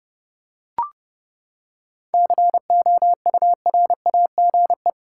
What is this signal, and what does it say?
Morse code, heard as a keyed tone, about 30 wpm, 700 Hz. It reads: COURAGE
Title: Seven Letter Words - 30wpm